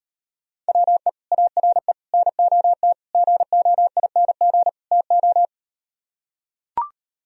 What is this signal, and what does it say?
Morse code, heard as a keyed tone, about 38 wpm, 700 Hz. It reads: WE ARE NOT GOING TO